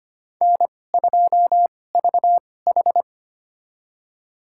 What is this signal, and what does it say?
Morse code, heard as a keyed tone, about 25 wpm, spaced as for 18 wpm, 700 Hz. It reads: N2VH
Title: USA Call Signs - 4 Characters - 2x Character Spacing - 25wpm